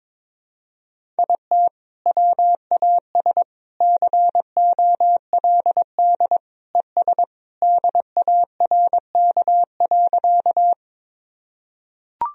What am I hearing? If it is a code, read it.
Morse code, heard as a keyed tone, about 22 wpm, 700 Hz. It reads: IT WAS COLD ES DARK.
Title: Sentences from Top 500 Words and Abbreviations 22wpm